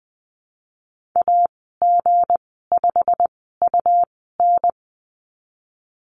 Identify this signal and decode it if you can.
Morse code, heard as a keyed tone, about 20 wpm, spaced as for 15 wpm, 700 Hz. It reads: AG5UN
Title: 173 Lesson 28g - Number 9 - Call Signs 20wpm